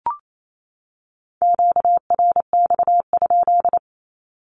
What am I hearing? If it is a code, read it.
Morse code, heard as a keyed tone, about 28 wpm, 700 Hz. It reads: QRX?
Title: Q-Codes 28wpm